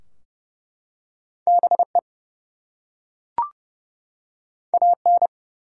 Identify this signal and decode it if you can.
Morse code, heard as a keyed tone, about 31 wpm, 700 Hz. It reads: BE AN
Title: Two Letter Words - 30wpm